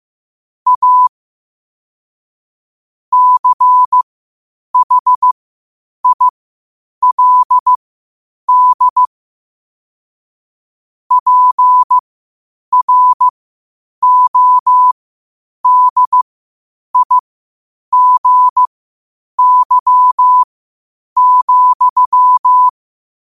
Morse code, heard as a keyed tone, about 15 wpm, spaced as for 8 wpm, 1000 Hz. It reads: A CHILD PRODIGY,
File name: qotdfiles / 8wpm